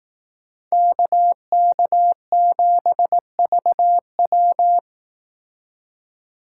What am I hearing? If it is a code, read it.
Morse code, heard as a keyed tone, about 18 wpm, 700 Hz. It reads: KK7VW
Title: Speed Racing - USA 2x2 Call Signs 15wpm